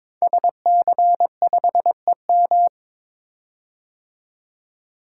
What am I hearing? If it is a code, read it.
Morse code, heard as a keyed tone, about 22 wpm, 700 Hz. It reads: SC5EM
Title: Code Groups - 5 Letter:Numbers - 22wpm